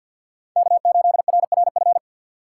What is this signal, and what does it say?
Morse code, heard as a keyed tone, about 47 wpm, 700 Hz. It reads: K8RRF
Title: USA Call Signs - 5 Characters - 2x Character Spacing - 50wpm